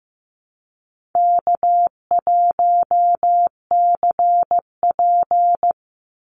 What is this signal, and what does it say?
Morse code, heard as a keyed tone, about 15 wpm, 700 Hz. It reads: K1CP
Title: Easy Call Signs Repeated 3x - 15wpm